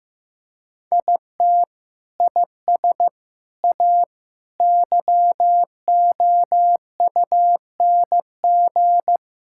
Morse code, heard as a keyed tone, about 15 wpm, 700 Hz. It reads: IT IS A YOUNG
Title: Sentences from Top 1000 Words - Review Part 6 - 15wpm